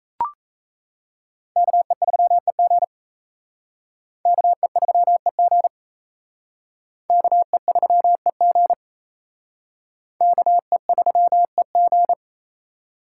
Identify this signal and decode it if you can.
Morse code, heard as a keyed tone, about 34 wpm, 700 Hz. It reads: KE3EG KE3EG KE3EG KE3EG
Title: Speed Racing - USA 2x2 Call Signs 28wpm